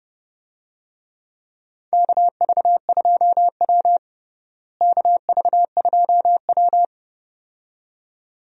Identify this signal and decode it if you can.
Morse code, heard as a keyed tone, about 30 wpm, 700 Hz. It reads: KV2W KV2W
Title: USA Call Signs - 4 Characters - Repeated 2x - 30wpm